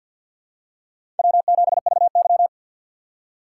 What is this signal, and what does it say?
Morse code, heard as a keyed tone, about 52 wpm, 700 Hz. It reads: W7VY
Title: USA Call Signs - 4 Characters - 6x Character Spacing - 50wpm